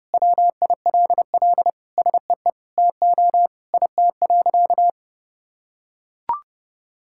Morse code, heard as a keyed tone, about 32 wpm, 700 Hz. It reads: WILL SEE TO IT.
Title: Sentences from Top 1000 Words - Review Part 5 - 30wpm